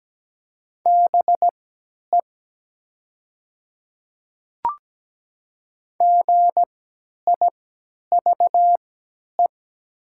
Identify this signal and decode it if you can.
Morse code, heard as a keyed tone, about 17 wpm, spaced as for 10 wpm, 700 Hz. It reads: BE GIVE
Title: 3x Character Spacing - Top 300 Words - 17wpm